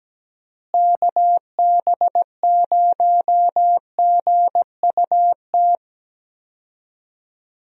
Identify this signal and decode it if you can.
Morse code, heard as a keyed tone, about 17 wpm, 700 Hz. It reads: KB0GUT
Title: USA Call Signs - 6 Characters - Repeated 3x - 17wpm